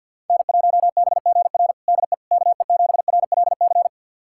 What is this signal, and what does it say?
Morse code, heard as a keyed tone, about 49 wpm, 700 Hz. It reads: N1BGR DE KE7RLC
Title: USA Callsign DE Callsign 50wpm